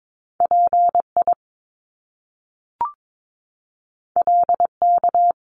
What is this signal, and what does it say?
Morse code, heard as a keyed tone, about 22 wpm, 700 Hz. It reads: PI LK
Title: Top Two Letter Combinations in Words 22wpm